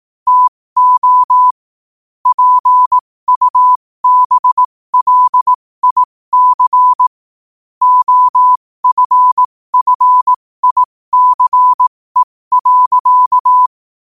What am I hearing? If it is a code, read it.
Morse code, heard as a keyed tone, about 18 wpm, spaced as for 15 wpm, 1000 Hz. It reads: TO PUBLIC OFFICE.
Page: qotd-at-15-wpm